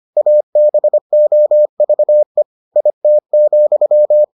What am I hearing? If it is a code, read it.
Morse code, heard as a keyed tone, about 25 wpm, 600 Hz. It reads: ABOVE IT,